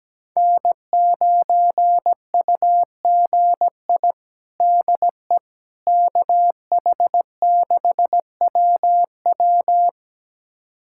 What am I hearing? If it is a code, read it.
Morse code, heard as a keyed tone, about 17 wpm, 700 Hz. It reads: N9UGI DE KH6WW
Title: USA Callsign DE Callsign - 5 Characters - 17wpm